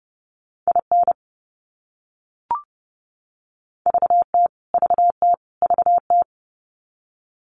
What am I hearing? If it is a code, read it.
Morse code, heard as a keyed tone, about 30 wpm, 700 Hz. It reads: IN VT VT VT
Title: US States Repeated 3x 30wpm